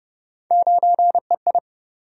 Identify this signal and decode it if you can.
Morse code, heard as a keyed tone, about 33 wpm, 700 Hz. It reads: 9EI